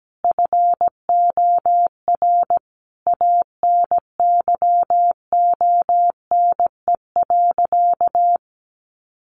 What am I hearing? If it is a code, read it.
Morse code, heard as a keyed tone, about 17 wpm, 700 Hz. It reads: FOR ANYONE.
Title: Sentences from Top 900 Words 17wpm